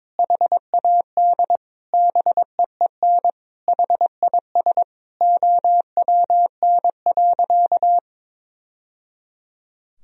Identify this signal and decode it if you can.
Morse code, heard as a keyed tone, about 22 wpm, 700 Hz. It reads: HAD BEEN HIS OWN.
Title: Sentences from Top 1000 Words - Review Part 2 - 22wpm